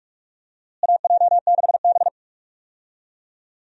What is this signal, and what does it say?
Morse code, heard as a keyed tone, about 46 wpm, 700 Hz. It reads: AJ6B
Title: USA Call Signs - 4 Characters - Repeated 3x - 45wpm